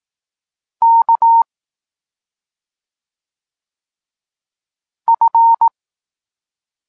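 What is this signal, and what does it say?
Morse code, heard as a keyed tone, about 18 wpm, 920 Hz. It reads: K F
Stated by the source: Polish Morse lesson teaching the letter C, from SP9MOA Niepołomicki Klub Krótkofalowców.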